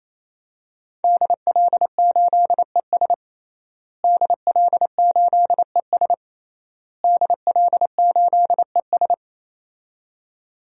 Morse code, heard as a keyed tone, about 27 wpm, 700 Hz. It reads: DL8ES DL8ES DL8ES
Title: DXCC-100 Call Signs - 5 Characters - Repeated 3x - 28wpm